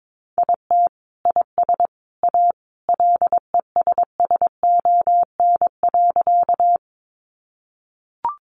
Morse code, heard as a keyed tone, about 22 wpm, 700 Hz. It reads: IT IS A LESSON.
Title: Sentences from Top 1200 Words 22wpm